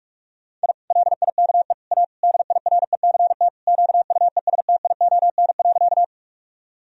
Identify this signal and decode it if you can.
Morse code, heard as a keyed tone, about 46 wpm, 700 Hz. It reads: I LIKE A DIRECT QUESTION.
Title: Sentences from Top 1000 Words - Review Part 2 - 45wpm